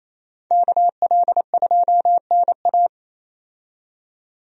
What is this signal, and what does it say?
Morse code, heard as a keyed tone, about 28 wpm, 700 Hz. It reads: KL2NA
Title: USA Call Signs - 5 Characters - 28wpm